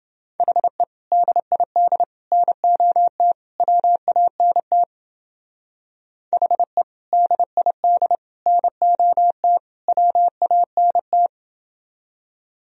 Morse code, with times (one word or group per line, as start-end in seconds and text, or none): 0.4-0.8 HE
1.1-2.0 DID
2.3-3.3 NOT
3.6-4.8 WANT
6.3-6.8 HE
7.1-8.2 DID
8.5-9.6 NOT
9.9-11.3 WANT